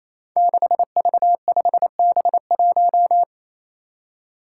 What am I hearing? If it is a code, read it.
Morse code, heard as a keyed tone, about 28 wpm, 700 Hz. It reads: 6V5B1
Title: Five Challenging Letter-Number Combinations 28wpm